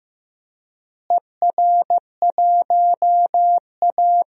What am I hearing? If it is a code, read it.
Morse code, heard as a keyed tone, about 15 wpm, 700 Hz. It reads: ER1A
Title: DXCC-100 Call Signs - 4 Characters - Repeated 3x - 15wpm